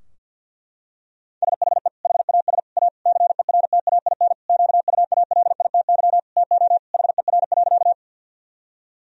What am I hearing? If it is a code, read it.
Morse code, heard as a keyed tone, about 51 wpm, 700 Hz. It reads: SHE HAS A CERTAIN QUALITY TO HER.
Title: Sentences from Top 1000 Words - Review Part 4 - 50wpm